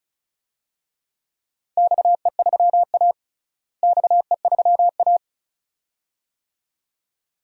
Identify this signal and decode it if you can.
Morse code, heard as a keyed tone, about 36 wpm, 700 Hz. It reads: XE3A XE3A
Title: DXCC-100 Call Signs - 4 Characters - Repeated 2x - 35wpm